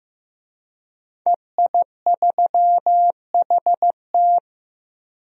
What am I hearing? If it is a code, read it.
Morse code, heard as a keyed tone, about 15 wpm, 700 Hz. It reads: EI3HT
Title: DXCC-100 Call Signs - 5 Characters - Repeated 3x - 15wpm